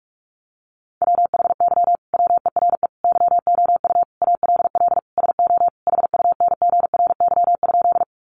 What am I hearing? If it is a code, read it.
Morse code, heard as a keyed tone, about 44 wpm, 700 Hz. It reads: WHY WERE YOU ALL SO HUNGRY?
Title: Sentences from Top 1400 Words 45wpm